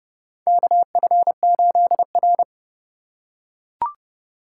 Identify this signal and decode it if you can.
Morse code, heard as a keyed tone, about 30 wpm, 700 Hz. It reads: KF8R
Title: USA Call Signs - 3x Character Spacing - 30wpm